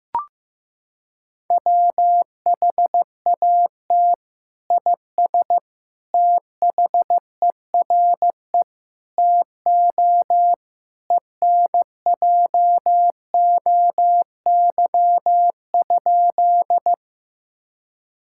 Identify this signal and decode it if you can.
Morse code, heard as a keyed tone, about 15 wpm, 700 Hz. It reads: WHAT IS THERE TO ENJOY?
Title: Sentences from Top 1000 Words - Review Part 5 - 15wpm